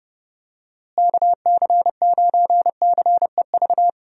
Morse code, heard as a keyed tone, about 30 wpm, 700 Hz. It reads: KC9CEV
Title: USA Call Signs - 6 Characters - Repeated 2x - 30wpm